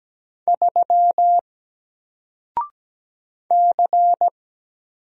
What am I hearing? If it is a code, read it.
Morse code, heard as a keyed tone, about 17 wpm, 700 Hz. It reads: 3 C
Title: Single Letter-Number Alt 17wpm